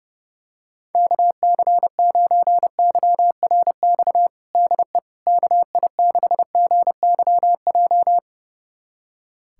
Morse code, heard as a keyed tone, about 30 wpm, 700 Hz. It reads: KC9YRX DE KI6GYJ
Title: USA Callsign DE Callsign 30wpm